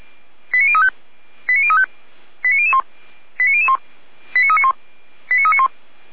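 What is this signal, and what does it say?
Five-tone selective call (ZVEI), heard as quick runs of five tones, tones of about 70 ms. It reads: 89035 89035 89001 89001 89391 89391
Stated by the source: Five-tone ZVEI selective call received on an SDR.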